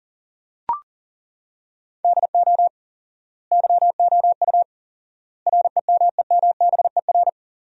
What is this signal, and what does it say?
Morse code, heard as a keyed tone, about 40 wpm, 700 Hz. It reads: DO YOU REMEMBER
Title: 4x Word-Spacing - Sentences from Top 400 Words - 40wpm